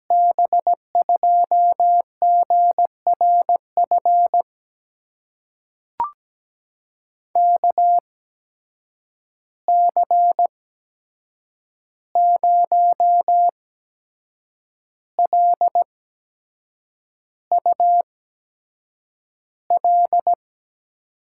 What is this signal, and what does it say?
Morse code, heard as a keyed tone, about 17 wpm, spaced as for 7 wpm, 700 Hz. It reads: B2GRF KC0LUL
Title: USA Call Signs - 6 Characters - 8x Character Spacing - 17wpm